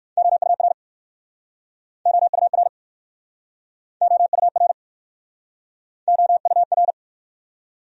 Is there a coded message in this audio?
Yes: Morse code from 0.2 to 6.9 s, heard as a keyed tone, about 63 wpm, 700 Hz. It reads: OUR OUR OUR OUR